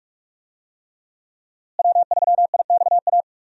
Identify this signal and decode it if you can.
Morse code, heard as a keyed tone, about 42 wpm, 700 Hz. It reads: W3IXA
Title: USA Call Signs - 5 Characters - Repeated 3x - 45wpm